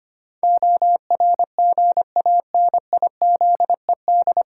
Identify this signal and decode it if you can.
Morse code, heard as a keyed tone, about 25 wpm, 700 Hz. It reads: ORGANIZED
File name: practice / 25wpm / 9 Letter Words - First 3 Letters Given - 25wpm